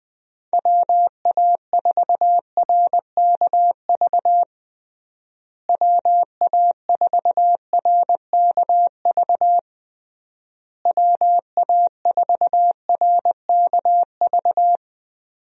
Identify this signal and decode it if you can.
Morse code, heard as a keyed tone, about 20 wpm, 700 Hz. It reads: WA4RKV WA4RKV WA4RKV
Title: USA Call Signs - 6 Characters - Repeated 3x - 20wpm